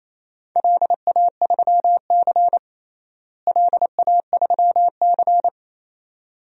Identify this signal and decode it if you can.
Morse code, heard as a keyed tone, about 28 wpm, 700 Hz. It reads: LA3C LA3C
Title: DXCC-100 Call Signs - 4 Characters - Repeated 2x - 28wpm